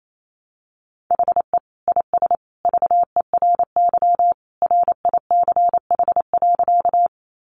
Morse code, heard as a keyed tone, about 28 wpm, 700 Hz. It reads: HE IS VERY RICH.
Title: Sentences from Top 1100 Words 28wpm